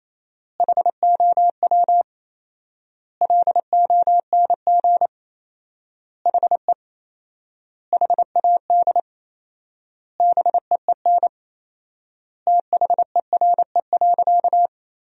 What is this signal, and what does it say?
Morse code, heard as a keyed tone, about 28 wpm, 700 Hz. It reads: HOW LONG HE HAD BEEN THERE.
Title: 4x Word-Spacing - Sentences from Top 300 Words - 28wpm